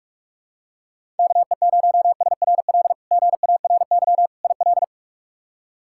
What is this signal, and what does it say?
Morse code, heard as a keyed tone, about 47 wpm, 700 Hz. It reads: KE0SRL GARY IL
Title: K1USN SST QSO Practice - 45wpm